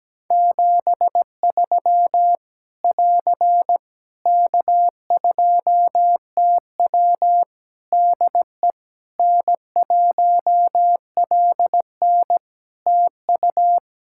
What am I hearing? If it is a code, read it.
Morse code, heard as a keyed tone, about 17 wpm, 700 Hz. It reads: 73 <AR> K2TW DE N1LN TU
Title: Practice QSO - Exchange 4 - Ending - 17wpm